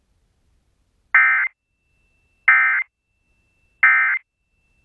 Emergency Alert System (SAME) tones, heard as harsh buzzing bursts. It is heard from 1.1 to 4.2 s.